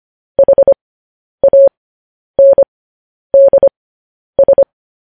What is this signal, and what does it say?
Morse code, heard as a keyed tone, about 25 wpm, spaced as for 10 wpm, 550 Hz. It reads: HANDS
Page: news headlines in morse code at 10 wpm